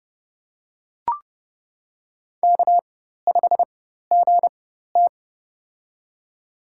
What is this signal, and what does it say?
Morse code, heard as a keyed tone, about 30 wpm, spaced as for 14 wpm, 700 Hz. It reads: K5GT
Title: USA Call Signs - 4 Characters - 4x Character Spacing - 30wpm